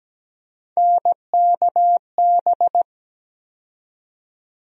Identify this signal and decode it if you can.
Morse code, heard as a keyed tone, about 17 wpm, 700 Hz. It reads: NKB
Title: USA Call Sign Postfixes - 3 Letters 17wpm